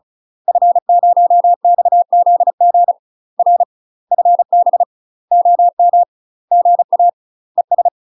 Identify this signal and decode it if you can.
Morse code, heard as a keyed tone, about 36 wpm, 700 Hz. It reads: F0XZG R FB OM GA ES
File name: UK MASTER QSOs TRAINING / 35WPM / UK MASTER QSOs 0035